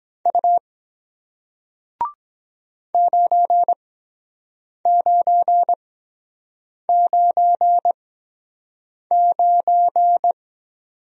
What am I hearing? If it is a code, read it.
Morse code, heard as a keyed tone, about 23 wpm, spaced as for 7 wpm, 700 Hz. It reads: U 9999